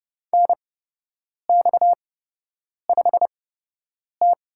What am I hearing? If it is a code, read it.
Morse code, heard as a keyed tone, about 31 wpm, spaced as for 8 wpm, 700 Hz. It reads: NX5T
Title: USA Call Signs - 8x Character Spacing - 30wpm